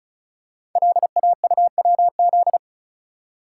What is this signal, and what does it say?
Morse code, heard as a keyed tone, about 36 wpm, 700 Hz. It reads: LAUWZ
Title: Code Groups - 5 Letter:Numbers - 35wpm